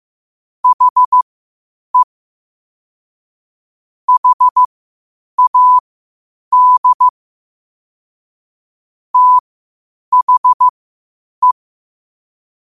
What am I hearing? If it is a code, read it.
Morse code, heard as a keyed tone, about 15 wpm, spaced as for 8 wpm, 1000 Hz. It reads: HE HAD THE
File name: 8wpm